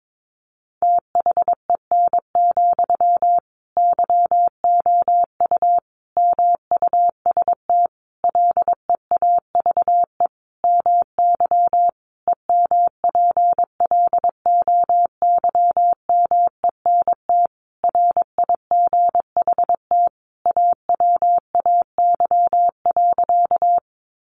Morse code, heard as a keyed tone, about 22 wpm, 700 Hz. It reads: THEN, YOU MUST LEAVE MY EMPLOYMENT RIGHT AWAY.
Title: Sentences from Top 1500 Words 22wpm